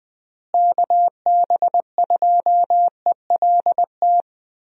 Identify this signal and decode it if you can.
Morse code, heard as a keyed tone, about 20 wpm, 700 Hz. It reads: KB2ELT
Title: USA Call Signs - 6 Characters - Repeated 2x - 20wpm